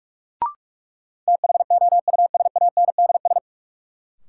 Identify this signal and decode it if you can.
Morse code, heard as a keyed tone, about 46 wpm, 700 Hz. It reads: THOUSANDS
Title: Sets of 4 Words - New Word Drill - 45wpm